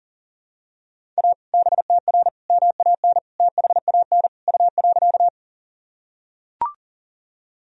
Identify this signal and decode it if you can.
Morse code, heard as a keyed tone, about 40 wpm, 700 Hz. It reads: A BTR MAN THAN U.
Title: Sentences from Top 400 Words and Abbreviations 40wpm